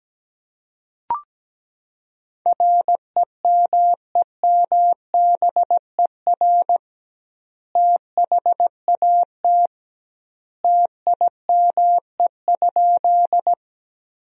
Morse code, heard as a keyed tone, about 17 wpm, 700 Hz. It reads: REMEMBER THAT TIME?
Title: 2x Word-Spacing - Sentences from Top 400 Words - 17wpm